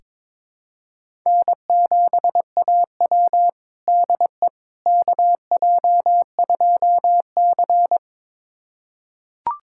Morse code, heard as a keyed tone, about 22 wpm, 700 Hz. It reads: N7AW DE KJ2C
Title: USA Callsign DE Callsign - 4 Characters - 22wpm